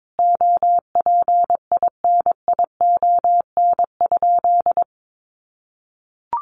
Morse code, heard as a keyed tone, about 22 wpm, 700 Hz. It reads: OPINION?
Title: Sentences from Top 1200 Words 22wpm